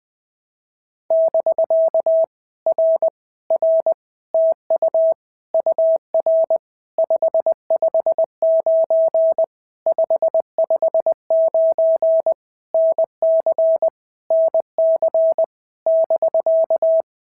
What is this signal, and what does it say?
Morse code, heard as a keyed tone, about 20 wpm, 650 Hz. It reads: <BK> R R TU UR 559 559 NC NC <BK>
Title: POTA QSO Practice - 20wpm